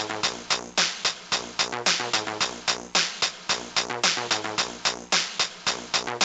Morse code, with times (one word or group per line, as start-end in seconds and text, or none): none